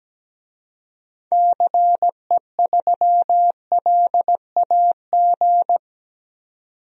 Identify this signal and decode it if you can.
Morse code, heard as a keyed tone, about 17 wpm, 700 Hz. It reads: CE3LAG
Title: DXCC-100 Call Signs - 6 Characters - 17wpm